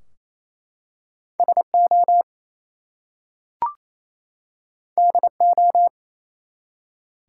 Morse code, heard as a keyed tone, about 28 wpm, 700 Hz. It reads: SO DO
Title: Two Letter Words - 28wpm